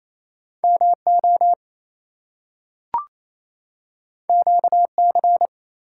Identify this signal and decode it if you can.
Morse code, heard as a keyed tone, about 28 wpm, 700 Hz. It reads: MO QC
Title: US States and Canadian Provinces 28wpm